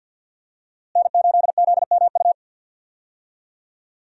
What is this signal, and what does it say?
Morse code, heard as a keyed tone, about 47 wpm, 700 Hz. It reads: N8ZMU
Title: USA Call Signs - 5 Characters - Repeated 3x - 50wpm